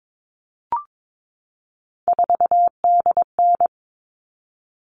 Morse code, heard as a keyed tone, about 22 wpm, 700 Hz. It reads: 4DN
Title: All Cut Numbers - 3-Digit 22wpm